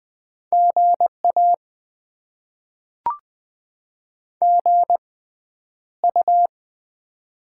Morse code, heard as a keyed tone, about 20 wpm, 700 Hz. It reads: GA G U
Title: Top 20 QSO Elements - 6x Character Spacing - 20wpm